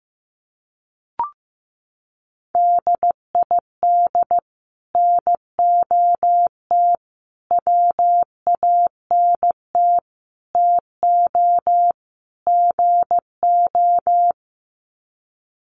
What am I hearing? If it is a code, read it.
Morse code, heard as a keyed tone, about 14 wpm, 700 Hz. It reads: DID NOT WANT TO GO
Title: Sets of 5 Words 15wpm